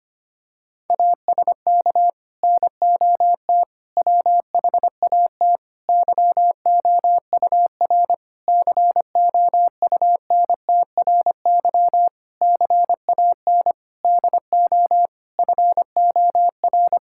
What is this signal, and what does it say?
Morse code, heard as a keyed tone, about 25 wpm, 700 Hz. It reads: ASK NOT WHAT YOUR COUNTRY CAN DO FOR
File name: practice / 25wpm / Sentences with 9-Word Phrases - 25wpm